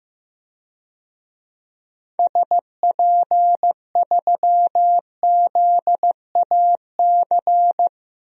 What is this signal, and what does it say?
Morse code, heard as a keyed tone, about 15 wpm, 700 Hz. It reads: SP3ZAC
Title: DXCC-100 Call Signs - 6 Characters - Repeated 3x - 15wpm